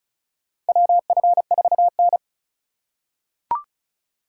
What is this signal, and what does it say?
Morse code, heard as a keyed tone, about 36 wpm, 700 Hz. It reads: WF4N
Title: USA Call Signs - 4 Characters - 4x Character Spacing - 35wpm